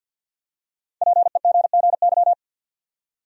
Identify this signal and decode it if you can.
Morse code, heard as a keyed tone, about 52 wpm, 700 Hz. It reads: PEGGY
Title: USA Common Names - Max 5 Letters - 50wpm